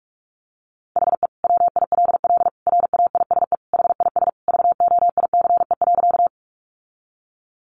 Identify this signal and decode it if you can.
Morse code, heard as a keyed tone, about 44 wpm, 700 Hz. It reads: HE WILL RAISE HIS VOICE.
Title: Sentences from Top 1400 Words 45wpm